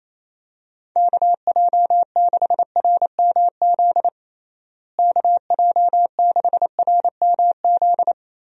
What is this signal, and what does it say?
Morse code, heard as a keyed tone, about 28 wpm, 700 Hz. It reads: KJ6RMZ KJ6RMZ
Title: USA Call Signs - 6 Characters - Repeated 3x - 28wpm